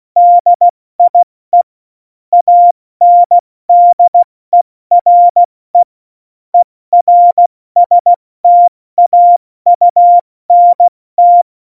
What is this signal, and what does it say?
Morse code, heard as a keyed tone, about 16 wpm, 700 Hz. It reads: DIE ANDERE ERSTAUNT
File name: Morsekurs / Witze / MP3, Witz 24 April 2018, 70 F80